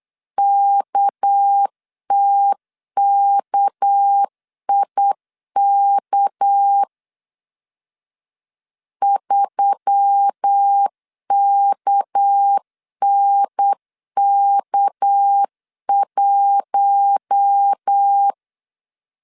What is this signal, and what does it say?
Morse code, heard as a keyed tone, about 8 wpm, 800 Hz. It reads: KTKIK 3KNK1
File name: LEZIONI / 6 PKX0 / 6 PK 8wpm 3003